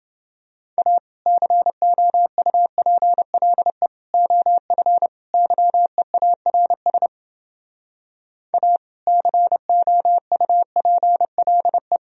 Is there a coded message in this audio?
Yes: Morse code from 0.8 to 12.0 s, heard as a keyed tone, about 30 wpm, 700 Hz. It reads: A COUPLE OF YEARS A COUPLE